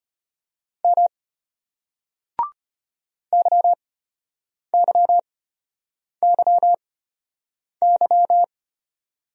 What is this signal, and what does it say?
Morse code, heard as a keyed tone, about 33 wpm, spaced as for 8 wpm, 700 Hz. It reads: M YYYY